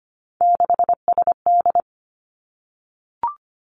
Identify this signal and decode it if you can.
Morse code, heard as a keyed tone, about 25 wpm, 700 Hz. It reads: 6SD
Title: Three Challenging Letter-Number Combinations 25wpm